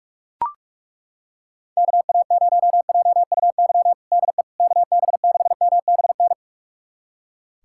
Morse code, heard as a keyed tone, about 41 wpm, 700 Hz. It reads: KA0JUY DE KB6MBN
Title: USA Callsign DE Callsign 45wpm